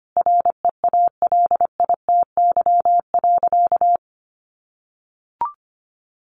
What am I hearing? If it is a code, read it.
Morse code, heard as a keyed tone, about 25 wpm, 700 Hz. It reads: REALITY.
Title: Sentences from Top 1500 Words 25wpm